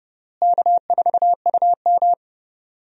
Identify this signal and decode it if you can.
Morse code, heard as a keyed tone, about 30 wpm, 700 Hz. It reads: K4UM
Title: USA Call Signs - 4 Characters - Repeated 3x - 30wpm